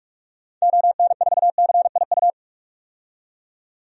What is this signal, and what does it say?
Morse code, heard as a keyed tone, about 46 wpm, 700 Hz. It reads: ON4CIU
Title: DXCC-100 Call Signs - 6 Characters - 45wpm